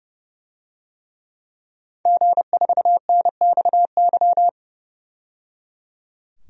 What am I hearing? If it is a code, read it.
Morse code, heard as a keyed tone, about 30 wpm, 700 Hz. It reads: G4NXY
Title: DXCC-100 Call Signs - 5 Characters - 30wpm